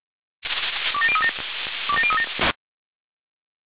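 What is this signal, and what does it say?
Five-tone selective call (ZVEI), heard as quick runs of five tones, tones of about 70 ms. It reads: 27037 27037
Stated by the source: ZVEI five-tone selective call received by SDR.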